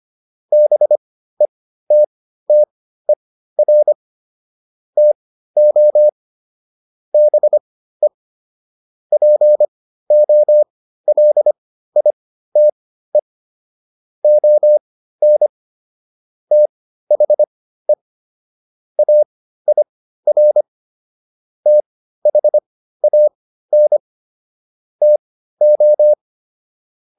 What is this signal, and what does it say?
Morse code, heard as a keyed tone, about 25 wpm, spaced as for 14 wpm, 600 Hz. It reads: BETTER TO BE POLITE ON THE AIR THAN TO